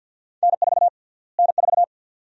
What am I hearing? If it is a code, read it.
Morse code, heard as a keyed tone, about 52 wpm, 700 Hz. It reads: N4 N4
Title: USA Call Sign Prefixes - Repeated 2x - 50wpm